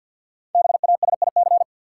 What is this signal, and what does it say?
Morse code, heard as a keyed tone, about 53 wpm, 700 Hz. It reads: BASIC
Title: Sentences from Top 1000 Words - Review Part 3 - New Word Drill - 50wpm